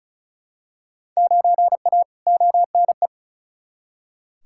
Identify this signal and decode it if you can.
Morse code, heard as a keyed tone, about 36 wpm, 700 Hz. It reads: 9A ONE
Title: Field Day Exchanges Encore - 35wpm